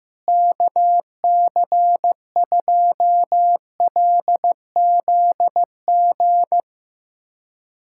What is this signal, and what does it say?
Morse code, heard as a keyed tone, about 15 wpm, 700 Hz. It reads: KC2LZG
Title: USA Callsign DE Callsign 15wpm